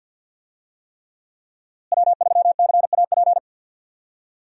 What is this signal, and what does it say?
Morse code, heard as a keyed tone, about 52 wpm, 700 Hz. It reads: W3CAP
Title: USA Call Signs - 5 Characters - Repeated 3x - 50wpm